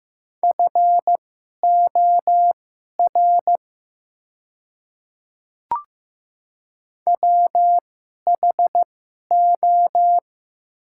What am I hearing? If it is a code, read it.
Morse code, heard as a keyed tone, about 15 wpm, spaced as for 11 wpm, 700 Hz. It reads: FOR WHO
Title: 2x Character Spacing - Top 100 Words - 15wpm